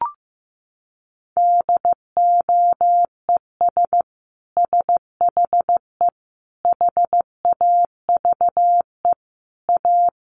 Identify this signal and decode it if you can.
Morse code, heard as a keyed tone, about 15 wpm, 700 Hz. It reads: DOES SHE HAVE A
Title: Sentences from Top 1500 Words 15wpm